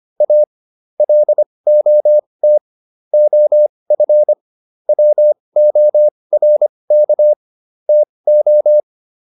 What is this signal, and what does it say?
Morse code, heard as a keyed tone, about 25 wpm, spaced as for 20 wpm, 600 Hz. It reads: A LOT OF WORK TO